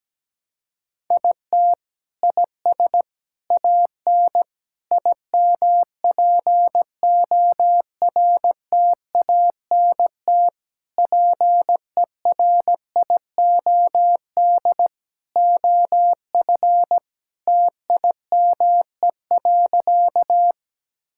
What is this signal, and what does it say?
Morse code, heard as a keyed tone, about 17 wpm, 700 Hz. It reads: IT IS AN IMPORTANT PERIOD OF TIME.
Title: Sentences from Top 1000 Words - Review Part 5 - 17wpm